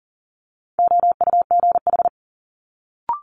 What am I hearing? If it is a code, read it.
Morse code, heard as a keyed tone, about 39 wpm, 700 Hz. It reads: OUGH
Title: Multi-letter Phonemes Quick Fire 40wpm